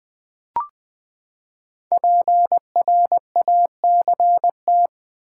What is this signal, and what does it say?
Morse code, heard as a keyed tone, about 20 wpm, 700 Hz. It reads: PRACT
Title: Build A Word - 5 Letters At A Time - 12-Letter Words - 20wpm